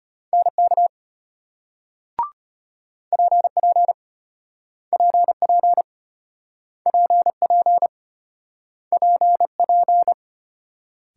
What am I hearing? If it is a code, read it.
Morse code, heard as a keyed tone, about 34 wpm, 700 Hz. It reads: NK PP PP PP PP